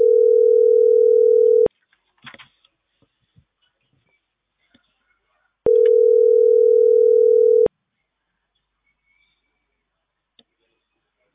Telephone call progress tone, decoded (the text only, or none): ringback tone